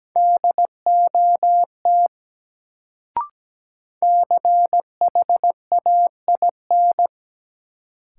Morse code, heard as a keyed tone, about 17 wpm, 700 Hz. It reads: DOT CHAIN
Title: Sentences from Top 1500 Words - New Word Drill - 17wpm